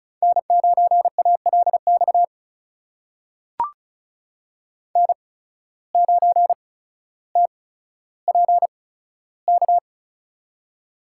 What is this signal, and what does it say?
Morse code, heard as a keyed tone, about 36 wpm, spaced as for 15 wpm, 700 Hz. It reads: N9ALX N9TPK
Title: USA Call Signs - 5 Characters - 8x Character Spacing - 35wpm